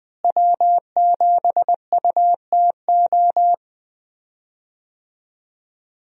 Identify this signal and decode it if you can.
Morse code, heard as a keyed tone, about 20 wpm, 700 Hz. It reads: W7UTO
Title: USA Call Signs - 5 Characters - Repeated 3x - 20wpm